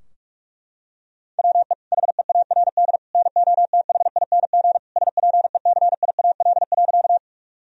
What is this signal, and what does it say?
Morse code, heard as a keyed tone, about 46 wpm, 700 Hz. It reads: WE HEARD NOTHING SPECIAL.
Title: Sentences from Top 1000 Words - Review Part 3 - 45wpm